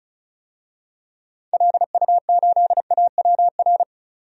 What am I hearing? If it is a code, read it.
Morse code, heard as a keyed tone, about 34 wpm, 700 Hz. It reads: LU8AWR
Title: DXCC-100 Call Signs - 6 Characters - 35wpm